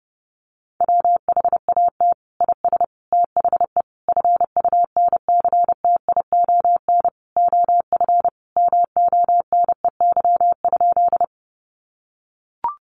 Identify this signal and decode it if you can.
Morse code, heard as a keyed tone, about 30 wpm, 700 Hz. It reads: WHAT IS THE FUNCTION OF MONEY?
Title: Sentences from Top 1100 Words 30wpm